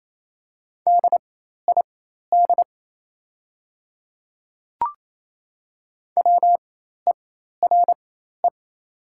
Morse code, heard as a keyed tone, about 28 wpm, spaced as for 13 wpm, 700 Hz. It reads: DID WERE
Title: 4x Character Spacing - Top 100 Words - 28wpm